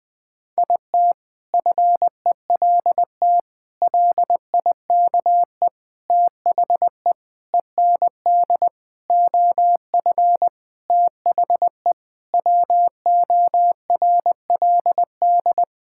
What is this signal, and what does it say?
Morse code, heard as a keyed tone, about 19 wpm, 700 Hz. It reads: IT FELT LIKE THE END OF THE WORLD
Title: Sentences with 8-Word Phrases - Spoken Beginning - 20wpm